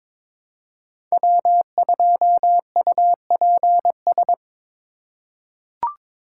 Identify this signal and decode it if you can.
Morse code, heard as a keyed tone, about 22 wpm, 700 Hz. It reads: W2UPS
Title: USA Call Signs - 2x Character Spacing - 22wpm